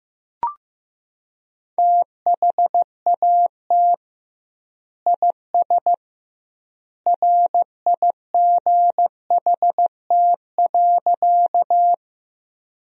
Morse code, heard as a keyed tone, about 15 wpm, 700 Hz. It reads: THAT IS RIGHT.
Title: 2x Word-Spacing - Sentences from Top 200 Words - 15wpm